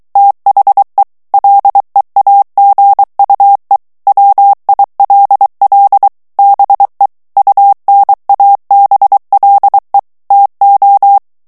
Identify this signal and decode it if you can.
Morse code, heard as a keyed tone, about 23 wpm, 800 Hz. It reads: THE LEAGUE WILL BE UNABLE TO
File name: ANGLAIS texte clair 3 vitesse 22 mots minute